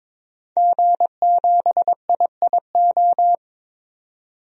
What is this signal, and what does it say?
Morse code, heard as a keyed tone, about 22 wpm, 700 Hz. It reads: G7IIO
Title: DXCC-100 Call Signs - 5 Characters - 22wpm